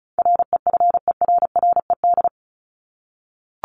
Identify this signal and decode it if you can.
Morse code, heard as a keyed tone, about 34 wpm, 700 Hz. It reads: REFERRED
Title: ED Suffix - Top 100 Words with Base Words 35wpm